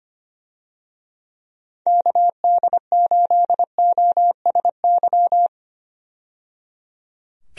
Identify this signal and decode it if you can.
Morse code, heard as a keyed tone, about 25 wpm, 700 Hz. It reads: KD8OSY
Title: USA Call Signs - 6 Characters - 25wpm